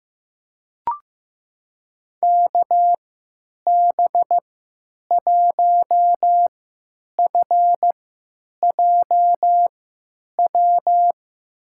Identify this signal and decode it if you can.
Morse code, heard as a keyed tone, about 15 wpm, spaced as for 9 wpm, 700 Hz. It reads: KB1FJW
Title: USA Call Signs - 6 Characters - 3x Character Spacing - 15wpm